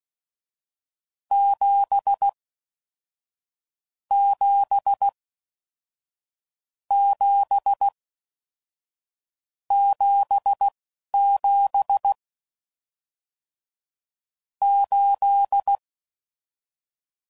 Morse code, heard as a keyed tone, about 16 wpm, spaced as for 11 wpm, 800 Hz. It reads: 7 7 7 77 8